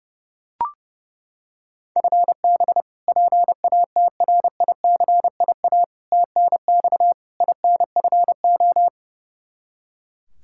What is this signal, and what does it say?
Morse code, heard as a keyed tone, about 31 wpm, 700 Hz. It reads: FB PATRICIA TNX INFO
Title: Practice QSO - Exchange 3 - More Info - 30wpm